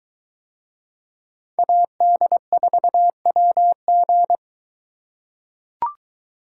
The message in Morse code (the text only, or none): AD4WG